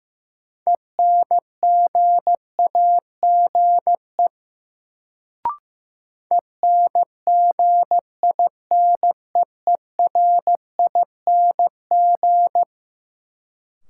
Morse code, heard as a keyed tone, about 15 wpm, 700 Hz. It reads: ENGAGE ENGINEERING
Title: Sentences from Top 1500 Words - New Word Drill - 15wpm